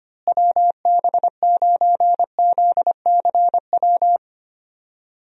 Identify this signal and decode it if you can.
Morse code, heard as a keyed tone, about 25 wpm, 700 Hz. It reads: WB9ZCW
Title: USA Call Signs - 6 Characters - Repeated 2x - 25wpm